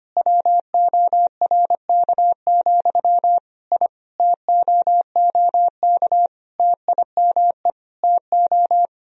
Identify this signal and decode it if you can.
Morse code, heard as a keyed tone, about 25 wpm, 690 Hz. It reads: WORK, I TOOK TIME TO
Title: Sentences from Top 1000 Words - Review Part 4 - 25wpm